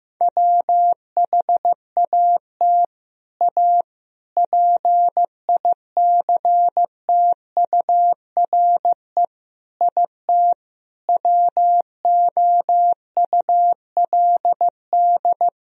Morse code, heard as a keyed tone, about 15 wpm, 700 Hz. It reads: WHAT A PICTURE IT WOULD
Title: Sentences from Top 1000 Words - Review Part 1 - 15wpm